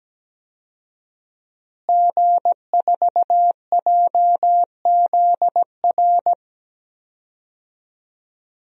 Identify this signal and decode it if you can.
Morse code, heard as a keyed tone, about 17 wpm, 700 Hz. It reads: G4JZR
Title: DXCC-100 Call Signs - 5 Characters - Repeated 3x - 17wpm